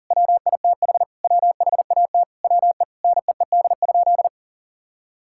Morse code, heard as a keyed tone, about 37 wpm, 700 Hz. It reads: WITH WHAT WE NEED?
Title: Sentences from Top 1000 Words - Review Part 5 - 40wpm